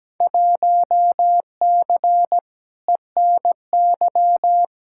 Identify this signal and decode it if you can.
Morse code, heard as a keyed tone, about 17 wpm, 700 Hz. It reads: 1C ENY
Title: Field Day Exchanges Encore - 17wpm